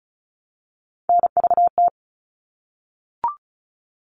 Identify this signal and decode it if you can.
Morse code, heard as a keyed tone, about 34 wpm, 700 Hz. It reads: NVT
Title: All Cut Numbers - 3-Digit 35wpm